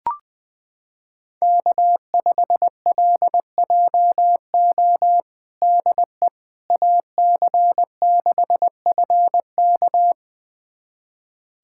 Morse code, heard as a keyed tone, about 20 wpm, 700 Hz. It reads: K5LJO DE AC6FK
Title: USA Callsign DE Callsign - 5 Characters - 20wpm